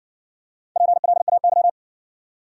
Morse code, heard as a keyed tone, about 58 wpm, 700 Hz. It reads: PLAY